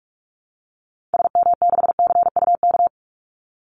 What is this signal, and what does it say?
Morse code, heard as a keyed tone, about 44 wpm, 700 Hz. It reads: SM6CUK
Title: Call Signs Repeated 2x - 45wpm